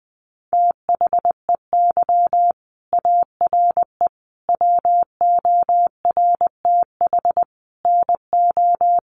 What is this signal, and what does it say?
Morse code, heard as a keyed tone, about 20 wpm, 700 Hz. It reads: THEY ARE WORTH NO
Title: Sentences from Top 800 Words 20wpm